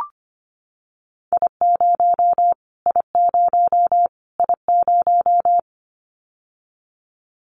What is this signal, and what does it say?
Morse code, heard as a keyed tone, about 25 wpm, 700 Hz. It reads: I0 I0 I0
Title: Callsign Prefixes for Top 25 DXCC - Repeated 3x - 25wpm